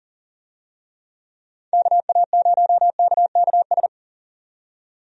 Morse code, heard as a keyed tone, about 40 wpm, 700 Hz. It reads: KA0KKS
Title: USA Call Signs - 6 Characters - 40wpm